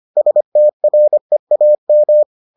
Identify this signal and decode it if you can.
Morse code, heard as a keyed tone, about 25 wpm, 600 Hz. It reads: STREAM